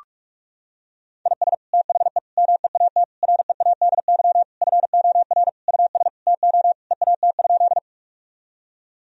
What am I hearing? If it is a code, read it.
Morse code, heard as a keyed tone, about 46 wpm, 700 Hz. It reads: IS THE MEAT READY FOR US TO EAT?
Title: Sentences from Top 1000 Words - Review Part 3 - 45wpm